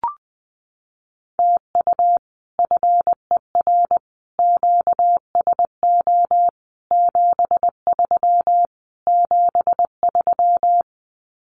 Morse code, heard as a keyed tone, about 20 wpm, 700 Hz. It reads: TU FER QSO 73 73
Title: Common QSO Phrases Encore 20wpm